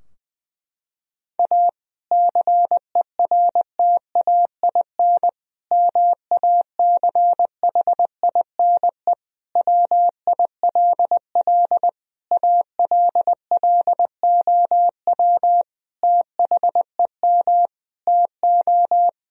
Morse code, heard as a keyed tone, about 20 wpm, 700 Hz. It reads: A CERTAIN MACHINE WILL ALLOW THEM TO
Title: Sentences from Top 1000 Words - Review Part 2 - 20wpm